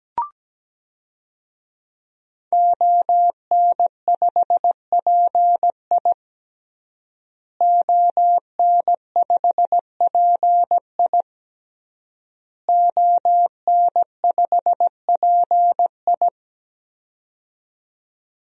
Morse code, heard as a keyed tone, about 17 wpm, 700 Hz. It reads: ON5PI ON5PI ON5PI
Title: DXCC-100 Call Signs - 5 Characters - Repeated 3x - 17wpm